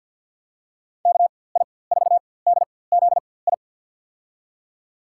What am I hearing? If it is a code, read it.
Morse code, heard as a keyed tone, about 46 wpm, spaced as for 23 wpm, 700 Hz. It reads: KI4DZI
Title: USA Call Signs - 6 Characters - 4x Character Spacing - 50wpm